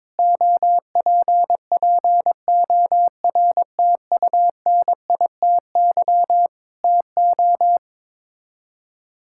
Morse code, heard as a keyed tone, about 22 wpm, 700 Hz. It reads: OPPORTUNITY TO
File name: practice / 22wpm / Sets of 8 Words - 22wpm